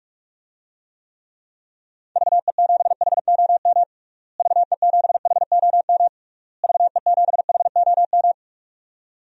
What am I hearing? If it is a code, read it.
Morse code, heard as a keyed tone, about 41 wpm, 700 Hz. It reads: VE7HOM VE7HOM VE7HOM
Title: DXCC-100 Call Signs - 6 Characters - Repeated 3x - 45wpm